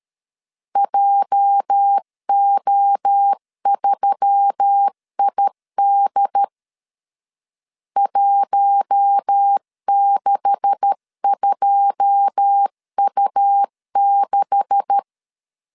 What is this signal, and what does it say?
Morse code, heard as a keyed tone, about 13 wpm, 800 Hz. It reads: JO3ID 162U6
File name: LEZIONI / 9 286 12wpm 2305